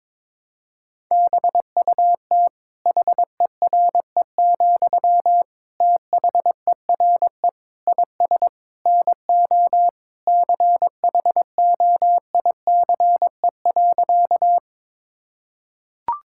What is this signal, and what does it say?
Morse code, heard as a keyed tone, about 21 wpm, 700 Hz. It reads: BUT HERE, THERE IS NO CHOICE.
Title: Sentences from Top 1000 Words - Review Part 6 - 22wpm